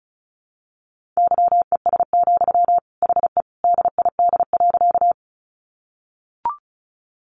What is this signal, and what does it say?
Morse code, heard as a keyed tone, about 34 wpm, 700 Hz. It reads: YES, HE DID.
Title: Sentences from Top 500 Words 35wpm